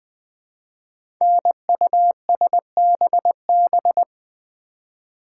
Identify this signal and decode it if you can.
Morse code, heard as a keyed tone, about 20 wpm, 700 Hz. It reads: NUSBB
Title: Code Groups - 5 Letters - 20wpm